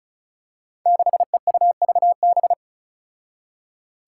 Morse code, heard as a keyed tone, about 36 wpm, 700 Hz. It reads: 6EUVB
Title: Five Challenging Letter-Number Combinations 35wpm